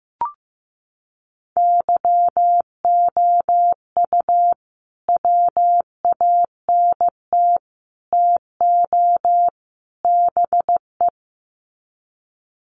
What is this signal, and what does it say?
Morse code, heard as a keyed tone, about 15 wpm, 700 Hz. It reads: YOU WANT TO BE
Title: Sets of 4 Words 15wpm